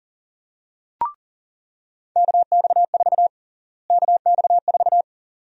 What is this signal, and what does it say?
Morse code, heard as a keyed tone, about 40 wpm, 700 Hz. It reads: KX4 KX4
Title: USA Call Sign Prefixes - Repeated 2x - 40wpm